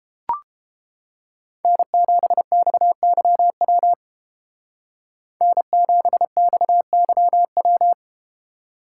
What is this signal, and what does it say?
Morse code, heard as a keyed tone, about 31 wpm, 700 Hz. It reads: N7XYW N7XYW